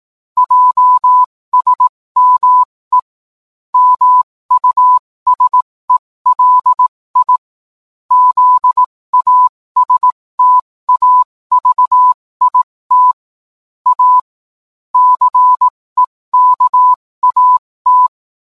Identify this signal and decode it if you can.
Morse code, heard as a keyed tone, about 18 wpm, spaced as for 15 wpm, 1000 Hz. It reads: JSME MUSELI ZASTAVIT A CEKAT